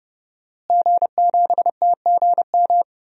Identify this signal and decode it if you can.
Morse code, heard as a keyed tone, about 29 wpm, 700 Hz. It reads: G7TGM
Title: DXCC-100 Call Signs - 5 Characters - Repeated 2x - 30wpm